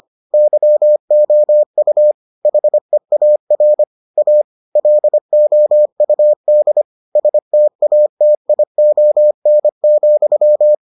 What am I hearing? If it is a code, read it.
Morse code, heard as a keyed tone, about 25 wpm, 600 Hz. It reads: YOU HEAR A LOUD STATION,